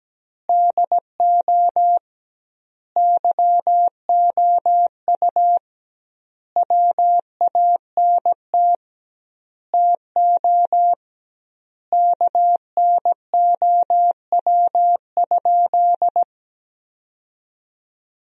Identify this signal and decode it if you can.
Morse code, heard as a keyed tone, about 17 wpm, 700 Hz. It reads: DO YOU WANT TO KNOW?
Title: 2x Word-Spacing - Sentences from Top 200 Words - 17wpm